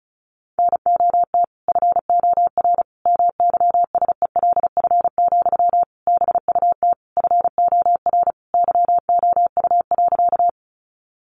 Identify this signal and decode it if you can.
Morse code, heard as a keyed tone, about 34 wpm, 700 Hz. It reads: NOT FOR MYSELF, BUT FOR YOU.
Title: Sentences from Top 1300 Words 35wpm